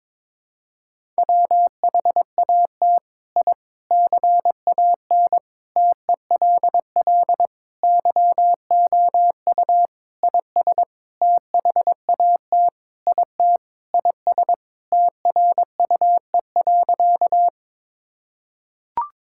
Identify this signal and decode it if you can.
Morse code, heard as a keyed tone, about 22 wpm, 700 Hz. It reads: WHAT I CAN TELL YOU IS THAT IT IS TRUE.
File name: Sentences with 7-Word Phrases - 22wpm